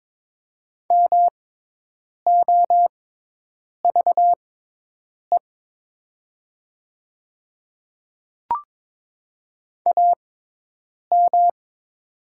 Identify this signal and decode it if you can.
Morse code, heard as a keyed tone, about 22 wpm, spaced as for 8 wpm, 700 Hz. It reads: MOVE AM
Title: 6x Character Spacing - Top 500 Words - 22wpm